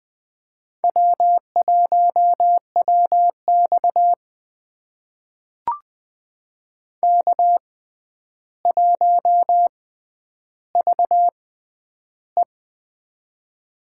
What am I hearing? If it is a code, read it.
Morse code, heard as a keyed tone, about 20 wpm, spaced as for 10 wpm, 700 Hz. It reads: W1WX K1VE
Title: USA Call Signs - 6x Character Spacing - 20wpm